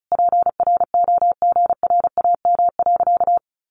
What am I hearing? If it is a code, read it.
Morse code, heard as a keyed tone, about 34 wpm, 700 Hz. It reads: PROGRAM.
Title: Sentences from Top 1500 Words 35wpm